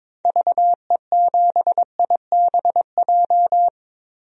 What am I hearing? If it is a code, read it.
Morse code, heard as a keyed tone, about 22 wpm, 700 Hz. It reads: VE7IBJ
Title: DXCC-100 Call Signs - 6 Characters - Repeated 3x - 22wpm